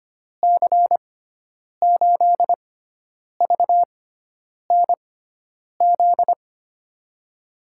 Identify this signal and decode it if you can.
Morse code, heard as a keyed tone, about 25 wpm, spaced as for 9 wpm, 700 Hz. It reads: C8VNZ
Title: USA Call Signs - 6 Characters - 6x Character Spacing - 25wpm